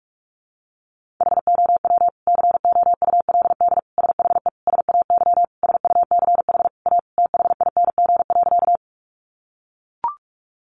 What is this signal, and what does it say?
Morse code, heard as a keyed tone, about 44 wpm, 700 Hz. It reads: HOW COULD SHE SAY SUCH A THING.
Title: Sentences from Top 200 Words 45wpm